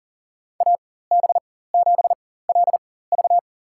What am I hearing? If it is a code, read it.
Morse code, heard as a keyed tone, about 40 wpm, spaced as for 19 wpm, 700 Hz. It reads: AB7LV
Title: USA Call Signs - 5 Characters - 4x Character Spacing - 40wpm